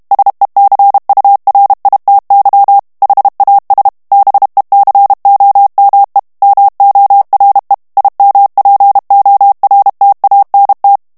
Morse code, heard as a keyed tone, about 32 wpm, 800 Hz. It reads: SECURITY HAS BECOME MORE IMPORTANT
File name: morse mp3 / ANGLAIS texte clair 1 vitesse 30 mots minute